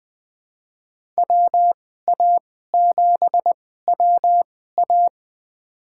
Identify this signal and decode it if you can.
Morse code, heard as a keyed tone, about 20 wpm, spaced as for 14 wpm, 700 Hz. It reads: WA7WA
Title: USA Call Signs - 5 Characters - 2x Character Spacing - 20wpm